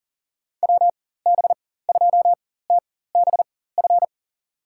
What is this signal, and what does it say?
Morse code, heard as a keyed tone, about 40 wpm, spaced as for 19 wpm, 700 Hz. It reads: WB2TBF
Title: USA Call Signs - 6 Characters - 4x Character Spacing - 40wpm